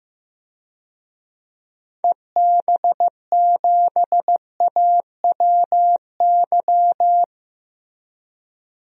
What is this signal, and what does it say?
Morse code, heard as a keyed tone, about 15 wpm, 700 Hz. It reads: EB7AWY
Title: DXCC-100 Call Signs - 6 Characters - Repeated 2x - 15wpm